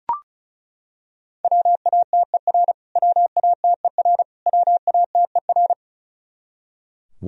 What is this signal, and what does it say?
Morse code, heard as a keyed tone, about 34 wpm, 700 Hz. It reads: WATER WATER WATER
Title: Top 100 Words Repeated 35wpm